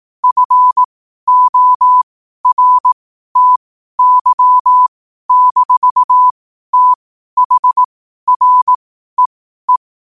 Morse code, heard as a keyed tone, about 18 wpm, spaced as for 12 wpm, 1000 Hz. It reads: FORTY-THREE
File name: qotdfiles / 12wpm